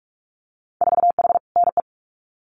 Morse code, heard as a keyed tone, about 46 wpm, 700 Hz. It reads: 4H NE
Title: Winter Field Day Exchanges Encore 45wpm